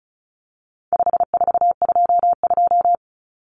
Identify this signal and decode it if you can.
Morse code, heard as a keyed tone, about 34 wpm, 700 Hz. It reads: 5422
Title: Numbers - 4-Digit 35wpm